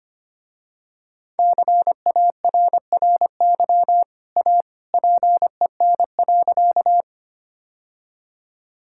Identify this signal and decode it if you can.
Morse code, heard as a keyed tone, about 25 wpm, 700 Hz. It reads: CARRY A PEN.
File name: Easy Sentences with Hints - 25wpm